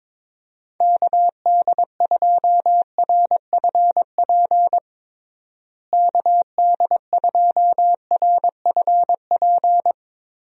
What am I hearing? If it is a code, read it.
Morse code, heard as a keyed tone, about 22 wpm, 700 Hz. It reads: KD2RFP KD2RFP
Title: USA Call Signs - 6 Characters - Repeated 3x - 22wpm